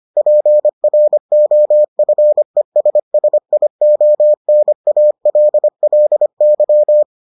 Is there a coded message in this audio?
Yes: Morse code from 0.2 to 7.0 s, heard as a keyed tone, about 25 wpm, 600 Hz. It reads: PROFESSIONALLY